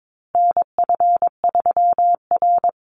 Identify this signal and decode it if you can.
Morse code, heard as a keyed tone, about 22 wpm, 700 Hz. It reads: NF3R
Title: Call Signs Repeated 2x - 22wpm